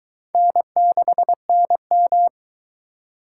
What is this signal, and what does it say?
Morse code, heard as a keyed tone, about 21 wpm, 700 Hz. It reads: N6NM